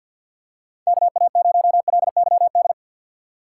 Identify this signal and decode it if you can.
Morse code, heard as a keyed tone, about 50 wpm, 700 Hz. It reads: KA0LYD
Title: USA Call Signs - 4x Character Spacing - 50wpm